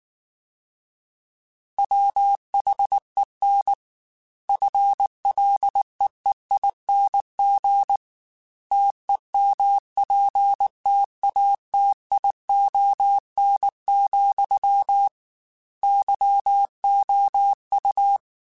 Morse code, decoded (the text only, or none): WHEN FLEEING TEMPTATION, YOU